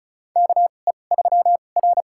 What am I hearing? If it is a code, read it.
Morse code, heard as a keyed tone, about 34 wpm, spaced as for 25 wpm, 700 Hz. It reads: KE3R
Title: USA Call Signs - 2x Character Spacing - 35wpm